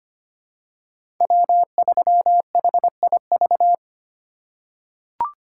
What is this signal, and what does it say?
Morse code, heard as a keyed tone, about 25 wpm, 700 Hz. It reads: W3HIV
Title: USA Call Signs - 5 Characters - 4x Character Spacing - 25wpm